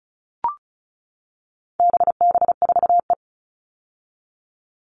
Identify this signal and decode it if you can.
Morse code, heard as a keyed tone, about 34 wpm, 700 Hz. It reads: BB4E
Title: All Cut Numbers - 4-Digit 35wpm